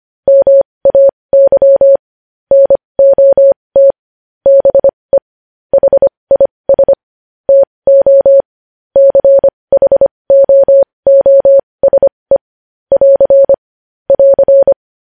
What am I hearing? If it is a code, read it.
Morse code, heard as a keyed tone, about 25 wpm, spaced as for 20 wpm, 550 Hz. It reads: MAY NOT BE HIS TO CHOOSE <AR> <AR>
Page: news headlines in morse code at 20 wpm